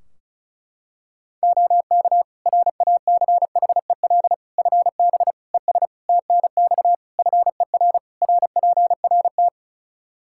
Morse code, heard as a keyed tone, about 36 wpm, 700 Hz. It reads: OK RACHEL FB ES TNX FER RPRT
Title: Practice QSO - Exchange 2 - Info - 35wpm